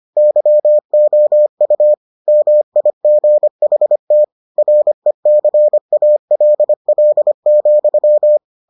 Morse code, heard as a keyed tone, about 25 wpm, 600 Hz. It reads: YOU MIGHT RECALL,